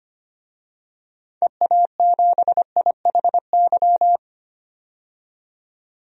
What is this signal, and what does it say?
Morse code, heard as a keyed tone, about 25 wpm, 700 Hz. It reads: EA7IHY
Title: DXCC-100 Call Signs - 6 Characters - 25wpm